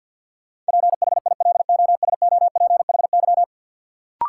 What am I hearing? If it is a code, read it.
Morse code, heard as a keyed tone, about 53 wpm, 700 Hz. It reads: PHILOSOPHY
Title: Sentences from Top 900 Words - New Word Drill - 50wpm